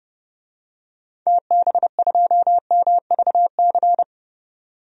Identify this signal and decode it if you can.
Morse code, heard as a keyed tone, about 30 wpm, 700 Hz. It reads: TB2MVC
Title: DXCC-100 Call Signs - 6 Characters - 30wpm